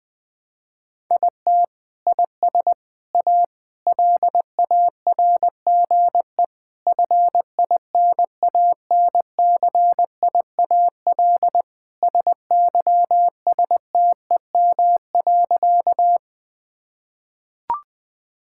Morse code, heard as a keyed tone, about 19 wpm, 700 Hz. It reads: IT IS A LARGE FINANCIAL SYSTEM.
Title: Sentences from Top 1000 Words - Review Part 5 - 20wpm